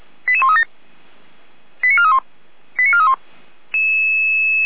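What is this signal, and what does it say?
Five-tone selective call (ZVEI), heard as quick runs of five tones, tones of about 70 ms. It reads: 88137 89431 89431
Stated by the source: ZVEI five-tone call received by SDR.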